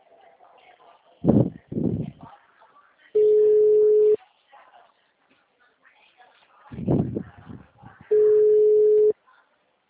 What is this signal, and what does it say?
Ringback tone (Europe, 425 Hz).
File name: out- -2014- - - .66